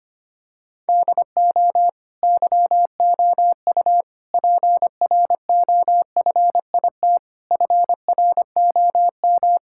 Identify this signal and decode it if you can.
Morse code, heard as a keyed tone, about 25 wpm, 700 Hz. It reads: DO YOU PROFIT FROM
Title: Sentences from Top 1000 Words - Review Part 5 - 25wpm